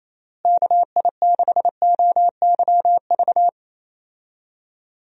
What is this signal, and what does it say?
Morse code, heard as a keyed tone, about 28 wpm, 700 Hz. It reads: KI6OYV
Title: USA Call Signs - 6 Characters - Repeated 2x - 28wpm